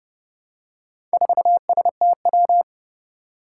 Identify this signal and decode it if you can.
Morse code, heard as a keyed tone, about 32 wpm, 700 Hz. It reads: 4STW
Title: Four Challenging Letter-Number Combinations 30wpm